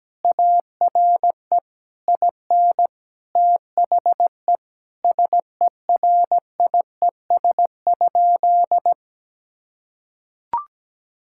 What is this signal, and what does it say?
Morse code, heard as a keyed tone, about 17 wpm, 700 Hz. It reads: ARE IN THE SERIES?
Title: Sentences from Top 1000 Words - Review Part 5 - 17wpm